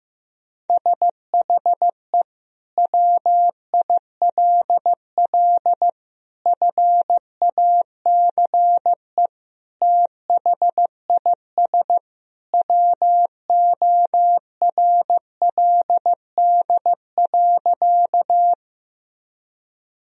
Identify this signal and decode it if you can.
Morse code, heard as a keyed tone, about 16 wpm, 700 Hz. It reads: SHE WILL FACE THIS WORLD.
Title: Sentences from Top 1000 Words - Review Part 2 - 15wpm